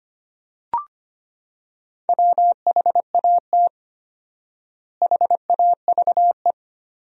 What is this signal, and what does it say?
Morse code, heard as a keyed tone, about 25 wpm, 700 Hz. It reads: WHAT HAVE
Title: 4x Word-Spacing - Sentences from Top 400 Words - 25wpm